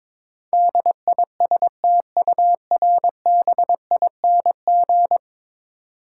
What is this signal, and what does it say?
Morse code, heard as a keyed tone, about 22 wpm, 700 Hz. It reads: DISTURBING
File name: practice / 22wpm / 10 Letter Words - First 3 Letters Given - 22wpm